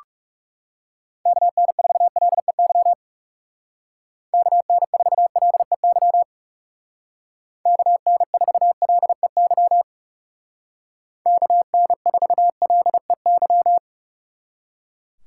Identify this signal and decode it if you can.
Morse code, heard as a keyed tone, about 37 wpm, 700 Hz. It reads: KN4LEY KN4LEY KN4LEY KN4LEY
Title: Speed Racing - USA 2x3 Call Signs 30wpm